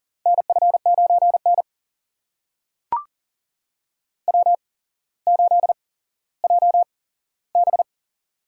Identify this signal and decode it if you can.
Morse code, heard as a keyed tone, about 40 wpm, spaced as for 17 wpm, 700 Hz. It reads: NF9N W8JB
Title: USA Call Signs - 4 Characters - 8x Character Spacing - 40wpm